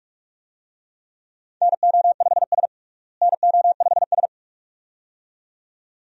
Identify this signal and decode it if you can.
Morse code, heard as a keyed tone, about 46 wpm, 700 Hz. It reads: NO5S NO5S
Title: USA Call Signs - 4 Characters - Repeated 2x - 45wpm